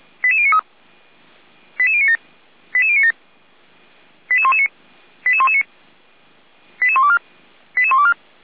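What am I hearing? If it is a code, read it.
Five-tone selective call (ZVEI), heard as quick runs of five tones, tones of about 80 ms. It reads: 88093 88097 88097 88119 88119 88124 88124
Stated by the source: ZVEI five-tone selective call, received by SDR.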